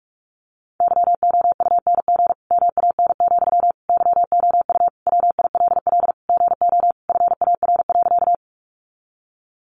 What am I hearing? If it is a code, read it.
Morse code, heard as a keyed tone, about 44 wpm, 700 Hz. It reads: YOUNG MAN, YOU WILL GO FAR.
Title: Sentences from Top 400 Words 45wpm